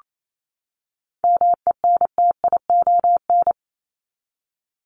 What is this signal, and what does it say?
Morse code, heard as a keyed tone, about 28 wpm, 700 Hz. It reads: MENTION
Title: Top 1501-1750 Words 28wpm